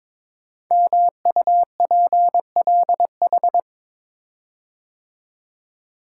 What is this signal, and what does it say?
Morse code, heard as a keyed tone, about 23 wpm, 700 Hz. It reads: MUPLH
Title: Code Groups - 5 Letters - 22wpm